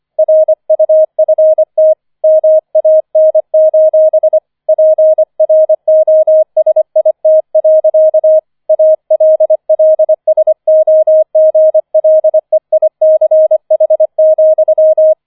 Morse code, heard as a keyed tone, about 25 wpm, 630 Hz. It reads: RUFT MAN: PROSIT. ALLSOGLEICH,
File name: 005 Fünfter Streich 25wpm